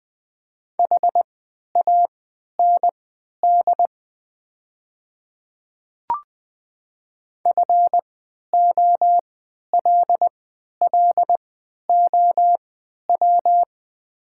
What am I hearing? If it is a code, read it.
Morse code, heard as a keyed tone, about 20 wpm, spaced as for 11 wpm, 700 Hz. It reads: HAND FOLLOW
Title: 3x Character Spacing - Top 200 Words - 20wpm